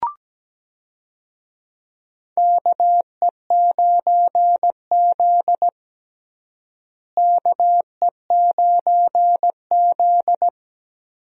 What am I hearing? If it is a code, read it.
Morse code, heard as a keyed tone, about 17 wpm, 700 Hz. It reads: KE9Z KE9Z
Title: USA Call Signs - 4 Characters - Repeated 2x - 17wpm